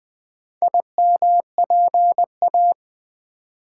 Morse code, heard as a keyed tone, about 20 wpm, 700 Hz. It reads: IMPA
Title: Build A Word - 2 Letters At A Time - 6-Letter Words - 20wpm